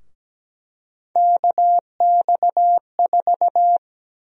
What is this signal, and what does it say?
Morse code, heard as a keyed tone, about 17 wpm, 700 Hz. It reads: KX4
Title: USA Call Sign Prefixes - 17wpm